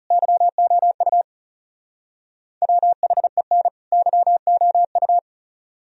Morse code, heard as a keyed tone, about 38 wpm, 700 Hz. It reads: YOU WHEN YOU